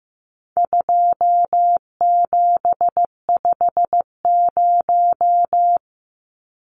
Morse code, heard as a keyed tone, about 15 wpm, 700 Hz. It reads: 2750
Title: Numbers - 4-Digit 15wpm